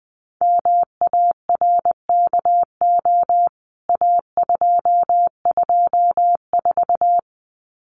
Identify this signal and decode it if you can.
Morse code, heard as a keyed tone, about 20 wpm, 700 Hz. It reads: MARKO A224
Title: CWT Exchange 20wpm